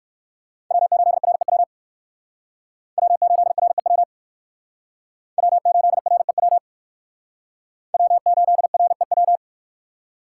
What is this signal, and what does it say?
Morse code, heard as a keyed tone, about 56 wpm, 700 Hz. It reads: W8REW W8REW W8REW W8REW